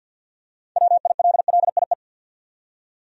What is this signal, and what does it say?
Morse code, heard as a keyed tone, about 47 wpm, 700 Hz. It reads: WILLIE
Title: USA Common Names - 50wpm